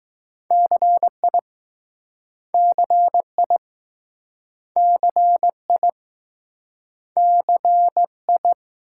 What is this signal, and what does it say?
Morse code, heard as a keyed tone, about 20 wpm, 700 Hz. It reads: CI CI CI CI